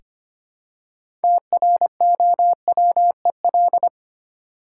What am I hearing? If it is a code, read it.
Morse code, heard as a keyed tone, about 25 wpm, 690 Hz. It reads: TROWEL
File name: practice / 25wpm / Easy Words with Hints - 25wpm